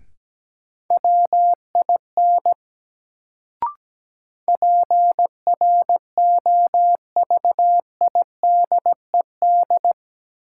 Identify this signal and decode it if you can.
Morse code, heard as a keyed tone, about 17 wpm, 700 Hz. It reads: WIN PROVIDED
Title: Sentences from Top 1500 Words - New Word Drill - 17wpm